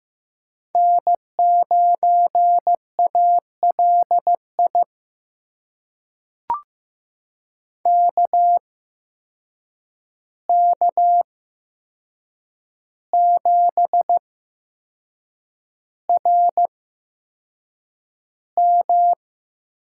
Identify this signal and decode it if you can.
Morse code, heard as a keyed tone, about 15 wpm, spaced as for 6 wpm, 700 Hz. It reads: N9ALI KK7RM
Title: USA Call Signs - 8x Character Spacing - 15wpm